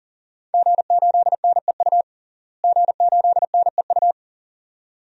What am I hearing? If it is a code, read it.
Morse code, heard as a keyed tone, about 37 wpm, 700 Hz. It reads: G8NEU G8NEU
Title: DXCC-100 Call Signs - 5 Characters - Repeated 2x - 40wpm